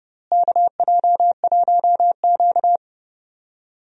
Morse code, heard as a keyed tone, about 30 wpm, 700 Hz. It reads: KJ1Q
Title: USA Call Signs - 4 Characters - Repeated 2x - 30wpm